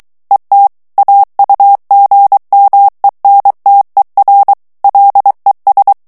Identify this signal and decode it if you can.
Morse code, heard as a keyed tone, about 23 wpm, 800 Hz. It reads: ET AUGMENTER LES
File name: morse mp3 / FRANCAIS TEXTE CLAIR 1 VITESSE 22 MOTS MINUTES